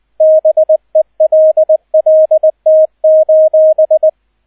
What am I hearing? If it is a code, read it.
Morse code, heard as a keyed tone, about 20 wpm, 630 Hz, with radio noise. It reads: BELLT: